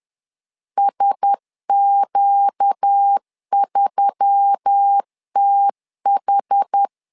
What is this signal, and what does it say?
Morse code, heard as a keyed tone, about 10 wpm, 800 Hz. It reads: SQ3TH
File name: LEZIONI / 5 LBVC / 5 L 10wpm 2259